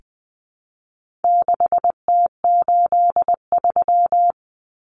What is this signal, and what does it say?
Morse code, heard as a keyed tone, about 20 wpm, 700 Hz. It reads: 6T83
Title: Common Cut Numbers - 4-Digit 20wpm